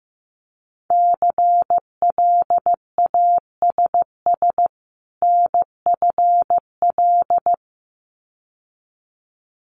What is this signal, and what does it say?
Morse code, heard as a keyed tone, about 15 wpm, 700 Hz. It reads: CLASS NFL
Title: Winter Field Day Exchanges 15wpm